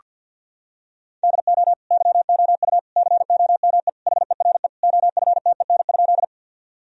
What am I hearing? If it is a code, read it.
Morse code, heard as a keyed tone, about 46 wpm, 700 Hz. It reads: DO YOU COME HERE OFTEN?
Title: Sentences from Top 1000 Words - Review Part 1 - 50wpm